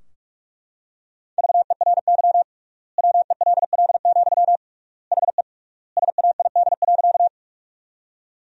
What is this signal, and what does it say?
Morse code, heard as a keyed tone, about 42 wpm, 700 Hz. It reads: VERY WELL, HE SAID.
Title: 3x Word-Spacing - Sentences from Top 200 Words - 45wpm